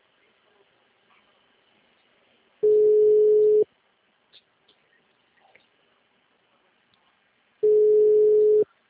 Ringback tone (Europe, 425 Hz).
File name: out- -2028- - - .55554